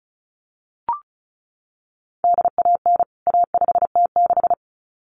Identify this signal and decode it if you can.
Morse code, heard as a keyed tone, about 34 wpm, 700 Hz. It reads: DAN A5T6
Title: CWT Exchange 35wpm